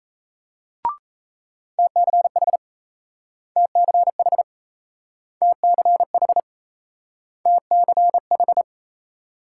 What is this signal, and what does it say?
Morse code, heard as a keyed tone, about 38 wpm, 700 Hz. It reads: TCH TCH TCH TCH